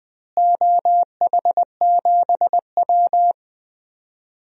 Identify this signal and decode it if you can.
Morse code, heard as a keyed tone, about 20 wpm, 700 Hz. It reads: OH7W
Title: DXCC-100 Call Signs - 4 Characters - Repeated 2x - 20wpm